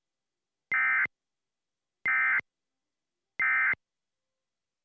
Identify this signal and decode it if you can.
Emergency Alert System (SAME) tones, heard as harsh buzzing bursts. It is heard from 0.7 to 3.7 s.